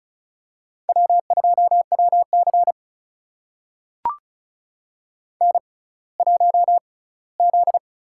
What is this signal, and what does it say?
Morse code, heard as a keyed tone, about 36 wpm, spaced as for 20 wpm, 700 Hz. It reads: W2WC N1Z
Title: USA Call Signs - 6x Character Spacing - 35wpm